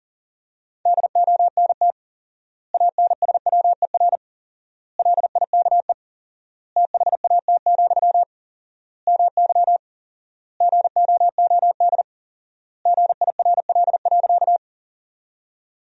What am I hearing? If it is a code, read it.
Morse code, heard as a keyed tone, about 40 wpm, 700 Hz. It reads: DONT ANSWER LIKE THAT, MY GOOD GIRL.
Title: 4x Word-Spacing - Sentences from Top 400 Words - 40wpm